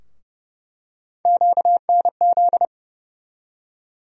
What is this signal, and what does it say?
Morse code, heard as a keyed tone, about 32 wpm, 700 Hz. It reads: QNZ
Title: USA Call Sign Postfixes - 3 Letters 30wpm